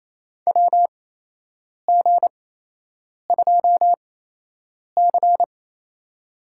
Morse code, heard as a keyed tone, about 28 wpm, spaced as for 8 wpm, 700 Hz. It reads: WG2C
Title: USA Call Signs - 4 Characters - 8x Character Spacing - 28wpm